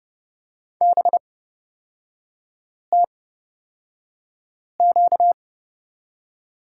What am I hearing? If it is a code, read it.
Morse code, heard as a keyed tone, about 30 wpm, spaced as for 5 wpm, 700 Hz. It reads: BTQ
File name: Lesson 38e - Letter q - Three Characters 30wpm